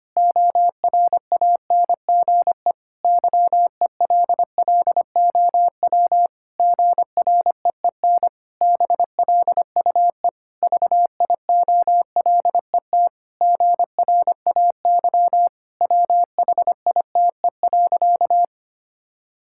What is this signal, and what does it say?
Morse code, heard as a keyed tone, about 25 wpm, 700 Hz. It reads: ORANGE YELLOW GREEN BLUE VIOLET GRAY WHITE.
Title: Pangram Challenge v2 - 25wpm